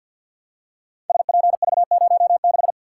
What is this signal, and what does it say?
Morse code, heard as a keyed tone, about 48 wpm, 700 Hz. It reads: SPV06
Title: Code Groups - 5 Letter:Numbers - 50wpm